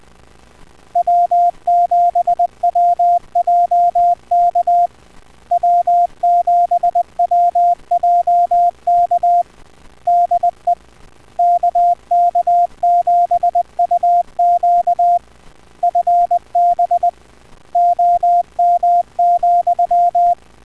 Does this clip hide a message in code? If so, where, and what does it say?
Yes: Morse code from 1.0 to 20.3 s, heard as a keyed tone, about 20 wpm, 700 Hz. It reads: W7WJK W7WJK DE KK7UQ FB OM,